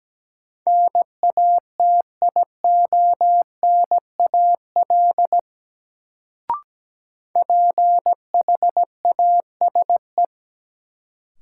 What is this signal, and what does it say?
Morse code, heard as a keyed tone, about 17 wpm, 700 Hz. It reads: NATIONAL PHASE
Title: Sentences from Top 1200 Words - New Word Drill Alt - 17wpm